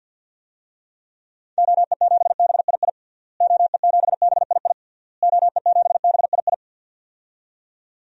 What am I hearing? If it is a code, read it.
Morse code, heard as a keyed tone, about 47 wpm, 700 Hz. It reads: OE7BII OE7BII OE7BII
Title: DXCC-100 Call Signs - 6 Characters - Repeated 3x - 50wpm